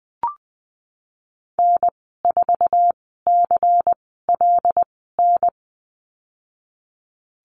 Morse code, heard as a keyed tone, about 20 wpm, spaced as for 15 wpm, 700 Hz. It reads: N4CLN
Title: 147 Lesson 24g - Letter y - Call Signs 20wpm